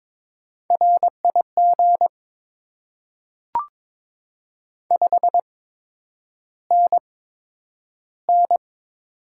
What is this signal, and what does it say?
Morse code, heard as a keyed tone, about 22 wpm, spaced as for 9 wpm, 700 Hz. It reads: RIG 5NN
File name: Top 20 QSO Elements - 8x Character Spacing - 22wpm